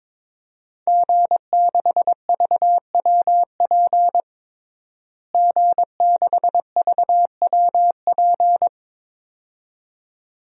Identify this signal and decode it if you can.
Morse code, heard as a keyed tone, about 22 wpm, 700 Hz. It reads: G6VWP G6VWP
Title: DXCC-100 Call Signs - 5 Characters - Repeated 2x - 22wpm